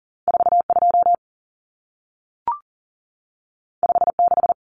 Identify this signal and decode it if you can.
Morse code, heard as a keyed tone, about 40 wpm, 700 Hz. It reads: 42 56
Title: Numbers - 2-Digit 40wpm